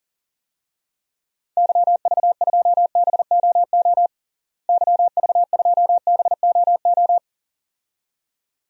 Morse code, heard as a keyed tone, about 37 wpm, 700 Hz. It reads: YV2BOO YV2BOO
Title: DXCC-100 Call Signs - 6 Characters - Repeated 2x - 40wpm